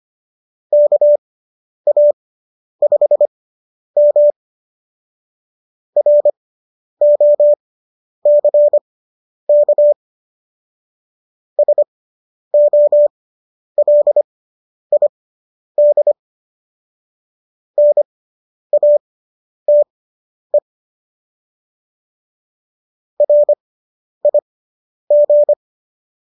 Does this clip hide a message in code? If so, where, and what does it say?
Yes: Morse code from 0.7 to 25.5 s, heard as a keyed tone, about 25 wpm, spaced as for 10 wpm, 590 Hz. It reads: KA5M ROCK SOLID NATE RIG